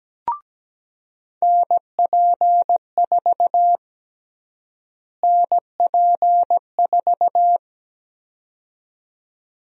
Morse code, heard as a keyed tone, about 17 wpm, 700 Hz. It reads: NP4 NP4
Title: USA Call Sign Prefixes - Repeated 2x - 17wpm